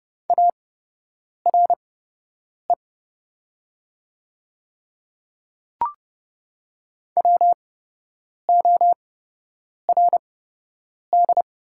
Morse code, heard as a keyed tone, about 30 wpm, spaced as for 8 wpm, 700 Hz. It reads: ARE WORD
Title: 8x Character Spacing - Top 100 Words - 30wpm